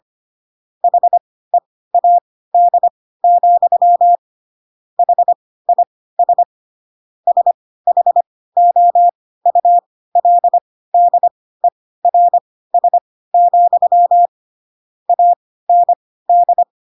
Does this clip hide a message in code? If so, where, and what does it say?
Yes: Morse code from 0.8 to 16.6 s, heard as a keyed tone, about 25 wpm, spaced as for 16 wpm, 700 Hz. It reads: HEAD, HIS SHOULDERS, AND